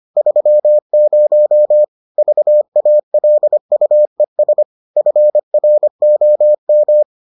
Morse code, heard as a keyed tone, about 25 wpm, 600 Hz. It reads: 30 VALUES FROM